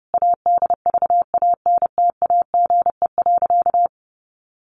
Morse code, heard as a keyed tone, about 30 wpm, 700 Hz. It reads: ADVANTAGE.
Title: Sentences from Top 1400 Words 30wpm